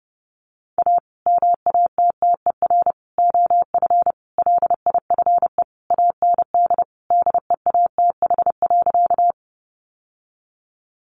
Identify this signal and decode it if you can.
Morse code, heard as a keyed tone, about 30 wpm, 700 Hz. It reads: A MATTER OF LIFE AND DEATH.
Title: Sentences from Top 1100 Words 30wpm